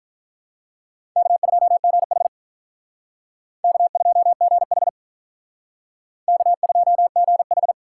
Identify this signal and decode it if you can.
Morse code, heard as a keyed tone, about 43 wpm, 700 Hz. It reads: K2GH K2GH K2GH
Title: Speed Racing - USA 1x2 Call Signs 35wpm